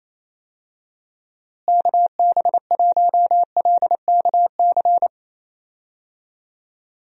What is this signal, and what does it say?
Morse code, heard as a keyed tone, about 27 wpm, 700 Hz. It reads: KB1LKC
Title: USA Call Signs - 6 Characters - Repeated 2x - 28wpm